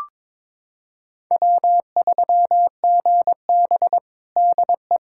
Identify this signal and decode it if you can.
Morse code, heard as a keyed tone, about 22 wpm, 700 Hz. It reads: W3GB DE
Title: USA Callsign DE Callsign - 4 Characters - 22wpm